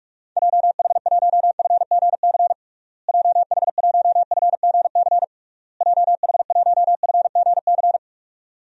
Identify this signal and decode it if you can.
Morse code, heard as a keyed tone, about 46 wpm, 700 Hz. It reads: JH1FGC JH1FGC JH1FGC
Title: DXCC-100 Call Signs - 6 Characters - Repeated 3x - 45wpm